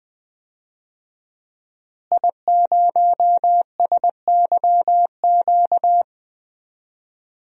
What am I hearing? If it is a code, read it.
Morse code, heard as a keyed tone, about 20 wpm, 700 Hz. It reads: I0SYQ
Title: DXCC-100 Call Signs - 5 Characters - Repeated 3x - 20wpm